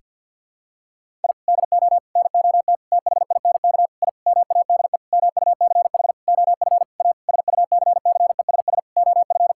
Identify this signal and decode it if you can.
Morse code, heard as a keyed tone, about 48 wpm, 700 Hz. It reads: I DO NOT THINK I MADE MUCH OF A SUCCESS OF
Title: Sentences from Top 1000 Words - Review Part 6 - 50wpm